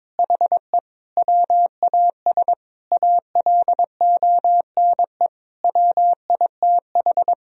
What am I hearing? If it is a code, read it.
Morse code, heard as a keyed tone, about 22 wpm, 700 Hz. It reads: HE WAS ALONE WITH
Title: Sentences from Top 1000 Words - Review Part 4 - 22wpm